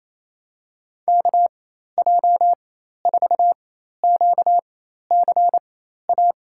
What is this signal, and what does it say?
Morse code, heard as a keyed tone, about 28 wpm, spaced as for 13 wpm, 700 Hz. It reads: KJ4QCA
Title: USA Call Signs - 4x Character Spacing - 28wpm